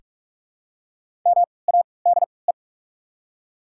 Morse code, heard as a keyed tone, about 44 wpm, spaced as for 25 wpm, 690 Hz. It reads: MADE
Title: 3x Character Spacing - Top 300 Words - 45wpm